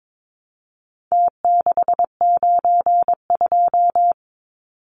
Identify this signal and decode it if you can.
Morse code, heard as a keyed tone, about 22 wpm, 700 Hz. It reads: T692
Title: Numbers - Contest Style - 4-Digit 22wpm